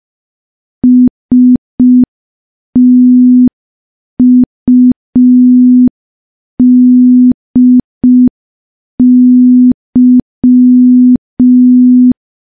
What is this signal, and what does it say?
Morse code, heard as a keyed tone, about 5 wpm, 250 Hz. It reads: STUDY
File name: morse / morse-training-05WPM